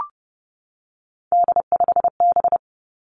Morse code, heard as a keyed tone, about 32 wpm, 700 Hz. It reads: D5B
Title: Three Challenging Letter-Number Combinations 30wpm